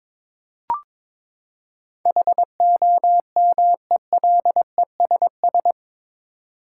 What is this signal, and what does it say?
Morse code, heard as a keyed tone, about 22 wpm, 700 Hz. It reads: HOMELESS